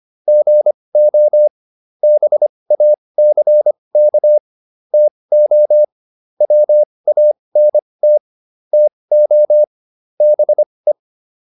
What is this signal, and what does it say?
Morse code, heard as a keyed tone, about 25 wpm, spaced as for 20 wpm, 600 Hz. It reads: GO BACK TO WANT TO BE